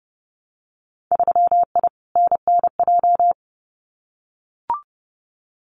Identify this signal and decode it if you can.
Morse code, heard as a keyed tone, about 30 wpm, 700 Hz. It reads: 3I NNJ
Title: Winter Field Day Exchanges 30wpm